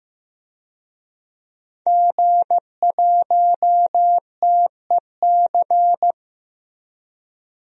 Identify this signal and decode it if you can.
Morse code, heard as a keyed tone, about 14 wpm, 700 Hz. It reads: G1TEC
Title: DXCC-100 Call Signs - 5 Characters - Repeated 3x - 15wpm